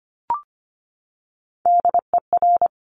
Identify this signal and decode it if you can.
Morse code, heard as a keyed tone, about 25 wpm, 700 Hz. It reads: DER
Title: Top Three Letter Combinations in Words 25wpm